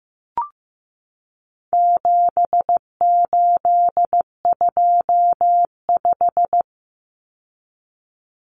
Morse code, heard as a keyed tone, about 15 wpm, 700 Hz. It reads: 7825
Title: Numbers - 4-Digit 15wpm